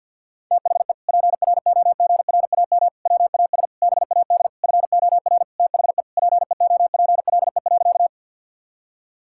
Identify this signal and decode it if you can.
Morse code, heard as a keyed tone, about 50 wpm, 700 Hz. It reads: THE PROGRAM WAS BAD FOR THE PEOPLE.
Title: Sentences from Top 1000 Words - Review Part 5 - 50wpm